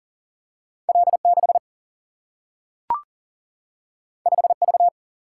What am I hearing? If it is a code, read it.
Morse code, heard as a keyed tone, about 39 wpm, 700 Hz. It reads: L6 5V
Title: Single-Letter Single-Number 40wpm